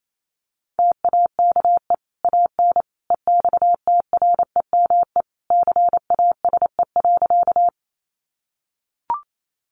Morse code, heard as a keyed tone, about 28 wpm, 700 Hz. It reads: TAKE AN EXTREME CASE.
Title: Sentences from Top 1300 Words 28wpm